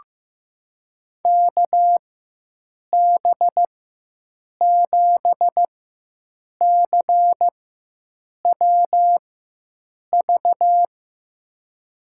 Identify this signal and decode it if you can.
Morse code, heard as a keyed tone, about 16 wpm, spaced as for 7 wpm, 700 Hz. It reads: KB7CWV
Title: USA Call Signs - 6 Characters - 4x Character Spacing - 15wpm